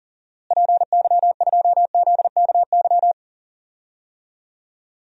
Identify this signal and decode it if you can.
Morse code, heard as a keyed tone, about 40 wpm, 700 Hz. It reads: PY2ZKY
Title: DXCC-100 Call Signs - 6 Characters - Repeated 3x - 40wpm